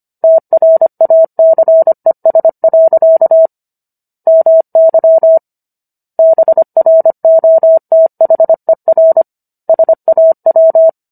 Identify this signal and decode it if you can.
Morse code, heard as a keyed tone, about 25 wpm, 650 Hz. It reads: TRACES. MY BROTHER SAW